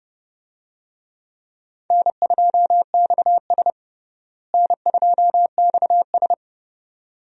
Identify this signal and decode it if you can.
Morse code, heard as a keyed tone, about 30 wpm, 700 Hz. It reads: N2XS N2XS
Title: USA Call Signs - 4 Characters - Repeated 2x - 30wpm